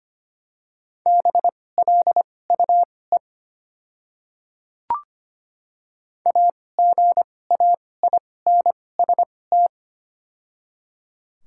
Morse code, heard as a keyed tone, about 25 wpm, spaced as for 18 wpm, 700 Hz. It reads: BLUE AGAINST
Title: 2x Character Spacing - Top 500 Words - 25wpm